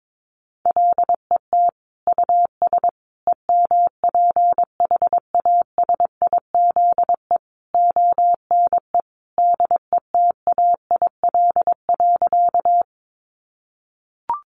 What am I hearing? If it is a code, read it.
Morse code, heard as a keyed tone, about 22 wpm, 700 Hz. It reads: LET US EMPHASIZE ONE DETAIL.
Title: Sentences from Top 1500 Words 22wpm